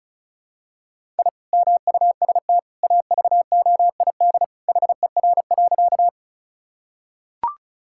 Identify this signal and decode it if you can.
Morse code, heard as a keyed tone, about 36 wpm, 700 Hz. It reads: I MUST AVOID HER.
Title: Sentences from Top 1000 Words - Review Part 6 - 35wpm